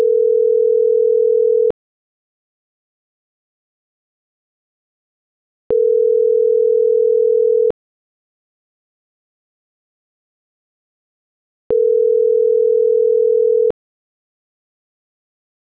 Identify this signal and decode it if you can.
Ringback tone (North America), 440 + 480 Hz.
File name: CENTRAL GOMSA / out francisco.cobos